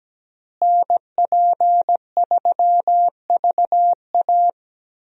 Morse code, heard as a keyed tone, about 17 wpm, 700 Hz. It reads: NP3VA
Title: USA Call Signs - 5 Characters - Repeated 2x - 17wpm